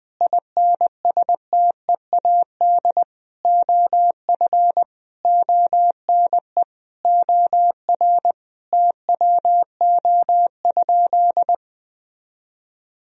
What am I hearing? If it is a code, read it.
Morse code, heard as a keyed tone, about 20 wpm, 700 Hz. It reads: INSTEAD OF ONE OR TWO?
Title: Sentences from Top 1000 Words - Review Part 4 - 20wpm